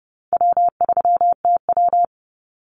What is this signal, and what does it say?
Morse code, heard as a keyed tone, about 30 wpm, 700 Hz. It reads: W3TW
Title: Easy Call Signs Repeated 3x - 30wpm